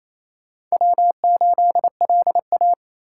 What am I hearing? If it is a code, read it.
Morse code, heard as a keyed tone, about 34 wpm, 700 Hz. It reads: W8LA
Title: Speed Racing - USA 1x2 Call Signs 28wpm